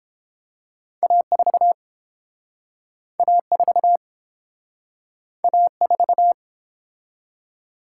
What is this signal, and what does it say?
Morse code, heard as a keyed tone, about 30 wpm, 700 Hz. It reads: A4 A4 A4